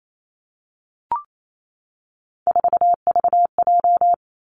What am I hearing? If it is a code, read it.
Morse code, heard as a keyed tone, about 28 wpm, 700 Hz. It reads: 4VJ
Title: Three Challenging Letter-Number Combinations 28wpm